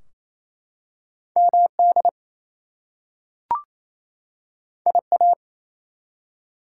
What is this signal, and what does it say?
Morse code, heard as a keyed tone, about 28 wpm, 700 Hz. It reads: MD IA
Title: US States and Canadian Provinces 28wpm